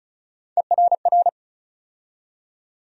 Morse code, heard as a keyed tone, about 36 wpm, 700 Hz. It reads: ERR
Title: USA Call Sign Postfixes - 3 Letters 35wpm